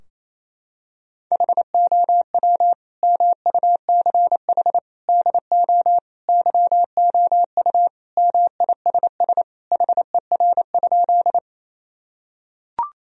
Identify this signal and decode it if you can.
Morse code, heard as a keyed tone, about 28 wpm, 700 Hz. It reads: HOW MUCH DO YOU MISS HER?
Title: Sentences from Top 1000 Words - Review Part 3 - 28wpm